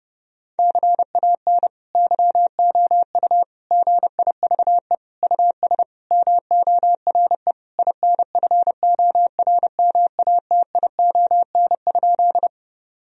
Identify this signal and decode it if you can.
Morse code, heard as a keyed tone, about 30 wpm, 700 Hz. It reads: CAN YOU GIVE US MORE INFORMATION?
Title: Sentences from Top 1000 Words - Review Part 3 - 30wpm